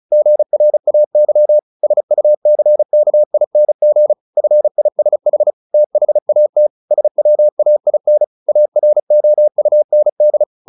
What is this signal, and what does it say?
Morse code, heard as a keyed tone, about 34 wpm, 600 Hz. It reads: GRAY SUCKING FISH THAT SWAIN AROUND
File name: the old man and the sea 2 35wpm